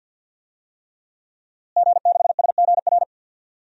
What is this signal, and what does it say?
Morse code, heard as a keyed tone, about 52 wpm, 700 Hz. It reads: G6SGR
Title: DXCC-100 Call Signs - 5 Characters - Repeated 3x - 50wpm